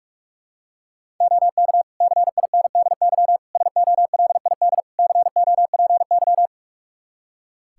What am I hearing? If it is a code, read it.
Morse code, heard as a keyed tone, about 41 wpm, 700 Hz. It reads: OK CINDY SOLID COPY
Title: K1USN SST QSO Practice 45wpm